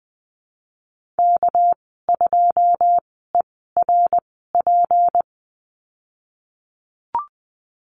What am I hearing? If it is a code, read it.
Morse code, heard as a keyed tone, about 20 wpm, spaced as for 15 wpm, 700 Hz. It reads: K2ERP
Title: 200 Lesson 32g - Letter k - Call Signs 20wpm